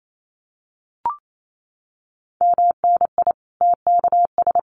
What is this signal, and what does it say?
Morse code, heard as a keyed tone, about 28 wpm, 700 Hz. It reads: MNI TKS
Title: Common QSO Phrases 28wpm